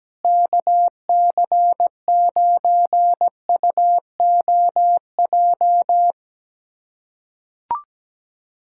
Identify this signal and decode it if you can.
Morse code, heard as a keyed tone, about 17 wpm, 700 Hz. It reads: KC9UOJ
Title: USA Callsign DE Callsign - 6 Characters - 17wpm